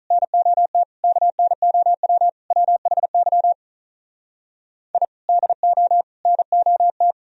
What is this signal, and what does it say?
Morse code, heard as a keyed tone, about 44 wpm, 700 Hz. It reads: NOT KNOW WHY I DO NOT